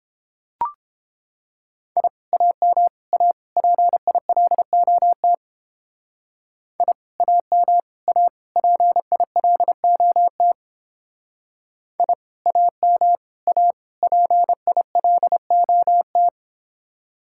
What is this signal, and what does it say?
Morse code, heard as a keyed tone, about 29 wpm, 700 Hz. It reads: I AM A PILOT I AM A PILOT I AM A PILOT